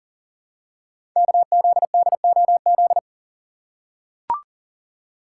Morse code, heard as a keyed tone, about 39 wpm, 700 Hz. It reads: KZDOZ
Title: Code Groups - 5 Letter:Numbers - 40wpm